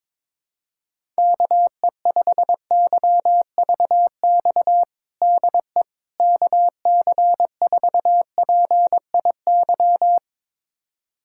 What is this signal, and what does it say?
Morse code, heard as a keyed tone, about 22 wpm, 700 Hz. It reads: KE5YVX DE KC4PIY
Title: USA Callsign DE Callsign - 6 Characters - 22wpm